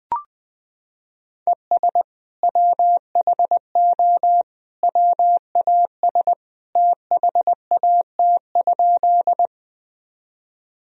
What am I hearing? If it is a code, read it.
Morse code, heard as a keyed tone, about 20 wpm, 700 Hz. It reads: ES WHO WAS THAT?
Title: Sentences from Top 100 Words and Abbreviations 20wpm